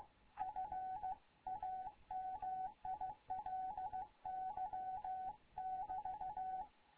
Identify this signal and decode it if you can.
Morse code, heard as a keyed tone, about 15 wpm, 750 Hz. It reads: FAMILY <BT>
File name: Morze-HA2MN / hangp / 2 11 GYAKORLAT